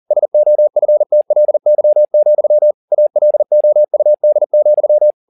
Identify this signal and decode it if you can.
Morse code, heard as a keyed tone, about 40 wpm, 600 Hz. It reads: SOFTLY, ALOUD,